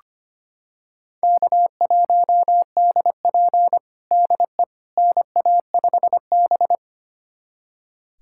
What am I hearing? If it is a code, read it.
Morse code, heard as a keyed tone, about 25 wpm, 700 Hz. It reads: K1DP DE NA5B
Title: USA Callsign DE Callsign - 4 Characters - 25wpm